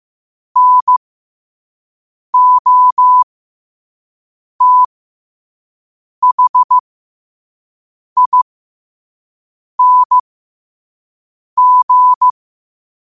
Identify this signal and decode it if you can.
Morse code, heard as a keyed tone, about 15 wpm, spaced as for 5 wpm, 1000 Hz. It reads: NOTHING
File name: qotdfiles / 5wpm